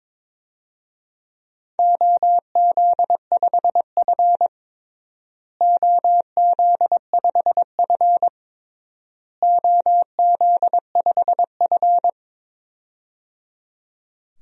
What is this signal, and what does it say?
Morse code, heard as a keyed tone, about 22 wpm, 700 Hz. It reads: OZ5F OZ5F OZ5F
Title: DXCC-100 Call Signs - 4 Characters - Repeated 3x - 22wpm